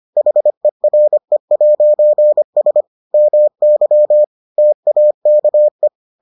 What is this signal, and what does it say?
Morse code, heard as a keyed tone, about 25 wpm, 600 Hz. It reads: HERE'S MY TAKE